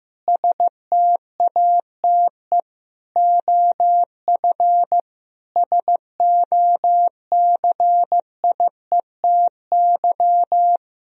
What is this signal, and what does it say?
Morse code, heard as a keyed tone, about 16 wpm, 700 Hz. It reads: STATE OF SOCIETY
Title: Sentences from Top 1000 Words - Review Part 4 - 15wpm